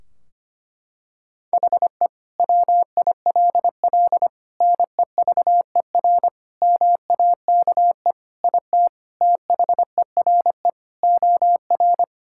Morse code, heard as a keyed tone, about 25 wpm, 700 Hz. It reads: HE WILL NEVER MAKE IT THERE OR
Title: Sentences from Top 1600 Words 25wpm